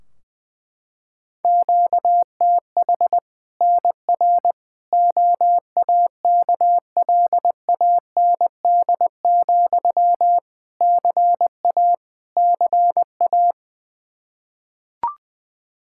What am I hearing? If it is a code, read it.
Morse code, heard as a keyed tone, about 21 wpm, 700 Hz. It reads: QTH NR OAKLAND, CA CA
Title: K1USN SST QSO Practice 20wpm